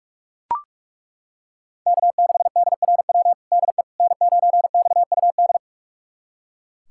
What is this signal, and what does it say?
Morse code, heard as a keyed tone, about 42 wpm, 700 Hz. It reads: K6DRW DE N9XUD
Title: USA Callsign DE Callsign - 5 Characters - 45wpm